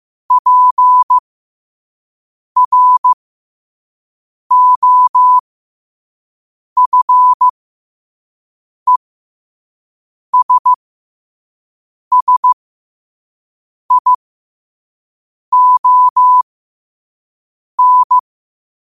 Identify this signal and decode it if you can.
Morse code, heard as a keyed tone, about 15 wpm, spaced as for 5 wpm, 1000 Hz. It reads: PROFESSION
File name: qotdfiles / 5wpm